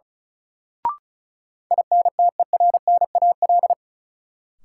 Morse code, heard as a keyed tone, about 36 wpm, 700 Hz. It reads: INTERNAL
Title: Sentences from Top 1400 Words - New Word Drill Alt - 35wpm